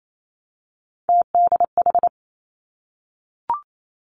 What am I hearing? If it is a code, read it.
Morse code, heard as a keyed tone, about 28 wpm, 700 Hz. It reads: TDH
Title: Three Challenging Letter-Number Combinations 28wpm